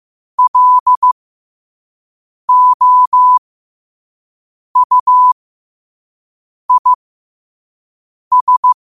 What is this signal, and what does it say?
Morse code, heard as a keyed tone, about 15 wpm, spaced as for 5 wpm, 1000 Hz. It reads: LOUIS
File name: qotdfiles / 5wpm